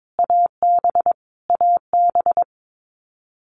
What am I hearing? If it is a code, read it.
Morse code, heard as a keyed tone, about 22 wpm, 700 Hz. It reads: AB AB
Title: Canadian Provinces and Territories - Repeated 3x - 22wpm